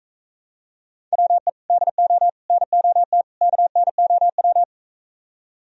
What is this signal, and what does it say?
Morse code, heard as a keyed tone, about 45 wpm, 700 Hz. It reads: WE DO NOT KNOW